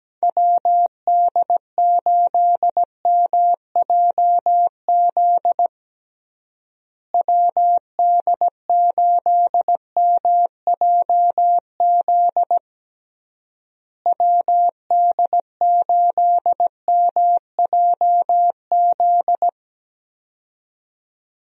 Morse code, heard as a keyed tone, about 17 wpm, 700 Hz. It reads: WD8MJZ WD8MJZ WD8MJZ
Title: USA Call Signs - 6 Characters - Repeated 3x - 17wpm